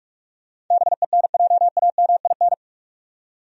Morse code, heard as a keyed tone, about 46 wpm, 700 Hz. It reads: BENJAMIN
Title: USA Common Names - 45wpm